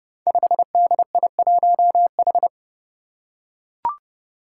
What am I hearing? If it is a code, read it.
Morse code, heard as a keyed tone, about 30 wpm, 700 Hz. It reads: 5DI1H
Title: Five Challenging Letter-Number Combinations 30wpm